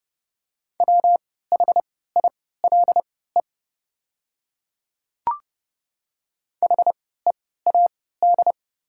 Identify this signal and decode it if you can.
Morse code, heard as a keyed tone, about 30 wpm, spaced as for 17 wpm, 700 Hz. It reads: WHILE HEAD
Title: 3x Character Spacing - Top 300 Words - 30wpm